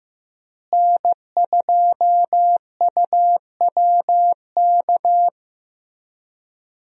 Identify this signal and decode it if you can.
Morse code, heard as a keyed tone, about 16 wpm, 700 Hz. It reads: N2UWK
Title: USA Call Signs - 5 Characters - Repeated 3x - 15wpm